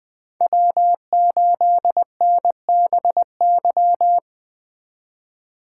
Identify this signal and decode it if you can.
Morse code, heard as a keyed tone, about 20 wpm, 700 Hz. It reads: W8NBY
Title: Speed Racing - USA 1x3 Call Signs 17wpm